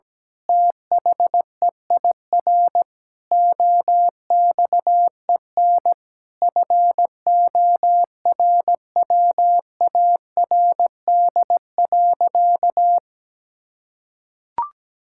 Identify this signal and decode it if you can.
Morse code, heard as a keyed tone, about 17 wpm, 700 Hz. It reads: THEIR OXEN FORWARD.
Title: Pangram Challenge v2 - 17wpm